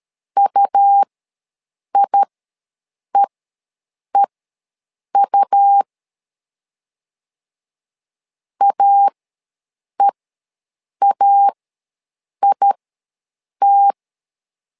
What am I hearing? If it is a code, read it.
Morse code, heard as a keyed tone, about 13 wpm, spaced as for 7 wpm, 800 Hz. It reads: UIEEU AEAIT
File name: LEZIONI / 2 UHDN / 2 UH ws 12wpm 1730